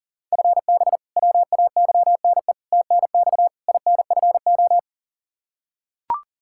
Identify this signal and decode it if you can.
Morse code, heard as a keyed tone, about 40 wpm, 700 Hz. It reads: FB WAYNE TNX INFO
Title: K1USN SST QSO Practice 40wpm